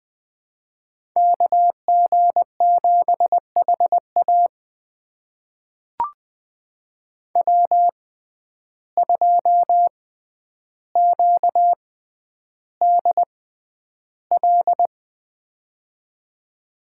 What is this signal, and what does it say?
Morse code, heard as a keyed tone, about 20 wpm, spaced as for 10 wpm, 700 Hz. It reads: KG7HA W2QDL
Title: USA Call Signs - 5 Characters - 6x Character Spacing - 20wpm